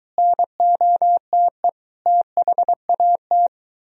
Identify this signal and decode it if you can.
Morse code, heard as a keyed tone, about 23 wpm, 700 Hz. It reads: NOTE THAT